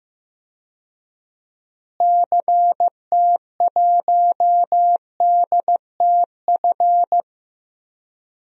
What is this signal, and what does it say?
Morse code, heard as a keyed tone, about 16 wpm, 700 Hz. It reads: CT1DTF
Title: DXCC-100 Call Signs - 6 Characters - Repeated 2x - 15wpm